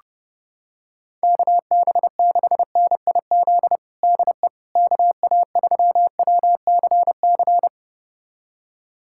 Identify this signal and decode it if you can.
Morse code, heard as a keyed tone, about 30 wpm, 690 Hz. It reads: KB6NIZ DE KA3WCC
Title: USA Callsign DE Callsign - 6 Characters - 30wpm